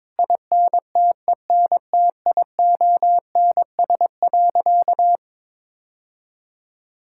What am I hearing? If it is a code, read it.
Morse code, heard as a keyed tone, about 22 wpm, 700 Hz. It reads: INTENTIONS.
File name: practice / 22wpm / Sentences with 8-Word Phrases - 22wpm